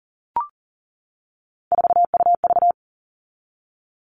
Morse code, heard as a keyed tone, about 40 wpm, 700 Hz. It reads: 4UV
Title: All Cut Numbers - 3-Digit 40wpm